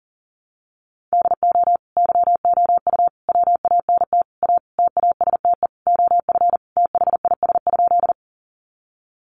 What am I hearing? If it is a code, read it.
Morse code, heard as a keyed tone, about 40 wpm, 700 Hz. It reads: DO YOU WANT A TASTE OF THIS?
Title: Sentences from Top 1100 Words 40wpm